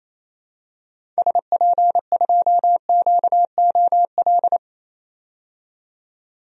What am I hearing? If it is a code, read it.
Morse code, heard as a keyed tone, about 28 wpm, 700 Hz. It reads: SP2QOL
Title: DXCC-100 Call Signs - 6 Characters - 28wpm